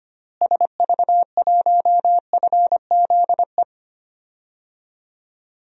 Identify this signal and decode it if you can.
Morse code, heard as a keyed tone, about 25 wpm, 700 Hz. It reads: SV1FZE
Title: DXCC-100 Call Signs - 6 Characters - Repeated 3x - 25wpm